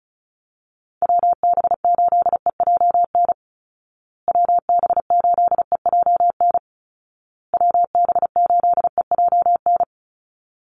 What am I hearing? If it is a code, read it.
Morse code, heard as a keyed tone, about 34 wpm, 700 Hz. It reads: WB8EJN WB8EJN WB8EJN
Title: Call Signs Repeated 2x - 35wpm